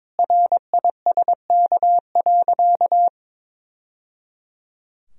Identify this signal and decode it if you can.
Morse code, heard as a keyed tone, about 22 wpm, 700 Hz. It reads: RISK.
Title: Sentences from Top 1000 Words - Review Part 4 - 22wpm